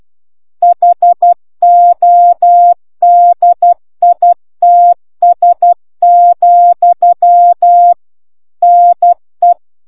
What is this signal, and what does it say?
Morse code, heard as a keyed tone, about 12 wpm, 700 Hz. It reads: HODITS, NE